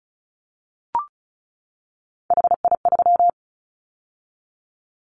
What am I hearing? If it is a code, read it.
Morse code, heard as a keyed tone, about 36 wpm, 700 Hz. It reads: HI3
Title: Callsign Prefixes for Top 100 DXCC - 35wpm